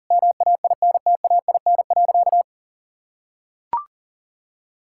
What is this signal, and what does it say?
Morse code, heard as a keyed tone, about 40 wpm, 700 Hz. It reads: MAINTAIN.
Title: Sentences from Top 1000 Words - Review Part 5 - 40wpm